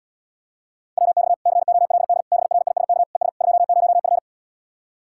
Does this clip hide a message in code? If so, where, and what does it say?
Yes: Morse code from 1.0 to 4.2 s, heard as a keyed tone, about 77 wpm, 700 Hz. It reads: WX COLD BRISK ES 20F